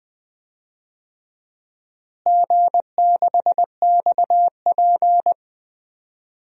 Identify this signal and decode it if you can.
Morse code, heard as a keyed tone, about 20 wpm, 700 Hz. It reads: G6XP
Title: DXCC-100 Call Signs - 4 Characters - Repeated 3x - 20wpm